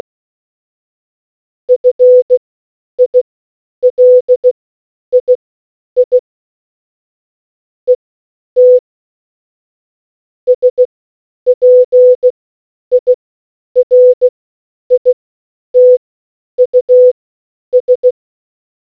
Morse code, heard as a keyed tone, about 16 wpm, spaced as for 9 wpm, 500 Hz. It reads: FILII ET SPIRITUS